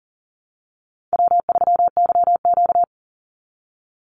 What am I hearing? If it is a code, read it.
Morse code, heard as a keyed tone, about 38 wpm, 700 Hz. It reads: W3YQ
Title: Call Signs - 40wpm